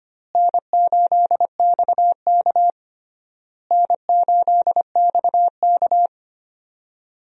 Morse code, heard as a keyed tone, about 25 wpm, 700 Hz. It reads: N8XK N8XK
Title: USA Call Signs - 4 Characters - Repeated 3x - 25wpm